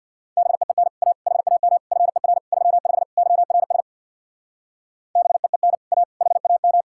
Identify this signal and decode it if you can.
Morse code, heard as a keyed tone, about 63 wpm, 700 Hz. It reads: BEEN A HAM FER 35 YRS BEEN A HAM